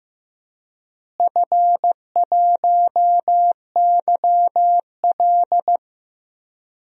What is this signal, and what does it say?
Morse code, heard as a keyed tone, about 15 wpm, 700 Hz. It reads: F1YL
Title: DXCC-100 Call Signs - 4 Characters - Repeated 3x - 15wpm